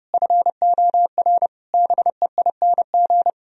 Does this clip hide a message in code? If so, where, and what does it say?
Yes: Morse code from 0.1 to 3.3 s, heard as a keyed tone, about 40 wpm, 700 Hz. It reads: FOR BEING